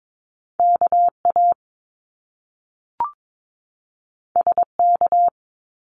Morse code, heard as a keyed tone, about 22 wpm, 700 Hz. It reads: KA SK
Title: Top Two Letter Combinations in Words 22wpm